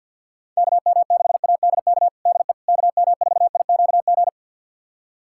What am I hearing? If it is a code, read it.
Morse code, heard as a keyed tone, about 48 wpm, 700 Hz. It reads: KM6ADK DE KM4IQG
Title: USA Callsign DE Callsign - 6 Characters - 50wpm